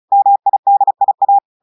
Morse code, heard as a keyed tone, about 36 wpm, 800 Hz. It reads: MIDIA
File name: audios / portugais / portugais texte en clair 1 vitesse 35 mots minutes